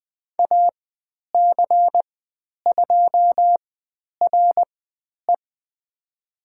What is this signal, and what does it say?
Morse code, heard as a keyed tone, about 20 wpm, spaced as for 10 wpm, 700 Hz. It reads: AC2RE
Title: 109 Lesson 18g - Letter m - Call Signs 20wpm